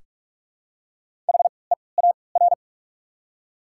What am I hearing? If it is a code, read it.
Morse code, heard as a keyed tone, about 44 wpm, spaced as for 25 wpm, 690 Hz. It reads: HEAR
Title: 3x Character Spacing - Top 300 Words - 45wpm